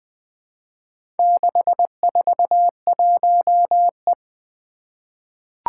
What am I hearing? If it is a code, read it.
Morse code, heard as a keyed tone, about 20 wpm, 700 Hz. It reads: 641E
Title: Four Letter-Number Combinations - 20wpm